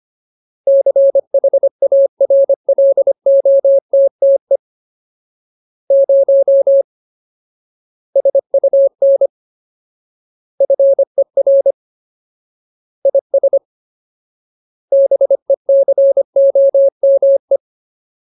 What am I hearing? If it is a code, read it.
Morse code, heard as a keyed tone, about 25 wpm, 550 Hz. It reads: CHARLOTTE 0 SUN FER IS BECOME
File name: Ultimate Word List / 25WPM Ultimate